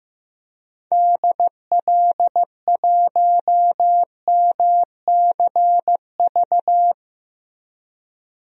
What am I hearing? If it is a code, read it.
Morse code, heard as a keyed tone, about 15 wpm, 700 Hz. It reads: DL1MCV
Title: DXCC-100 Call Signs - 6 Characters - Repeated 3x - 15wpm